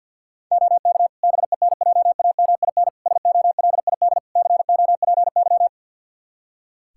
Morse code, heard as a keyed tone, about 52 wpm, 700 Hz. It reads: OK BENJAMIN SOLID COPY
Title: K1USN SST QSO Practice 50wpm